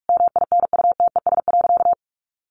Morse code, heard as a keyed tone, about 44 wpm, 700 Hz. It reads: MINUTES.
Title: A Sound Of Thunder 45wpm